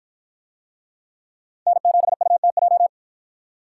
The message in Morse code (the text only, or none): N7UTJ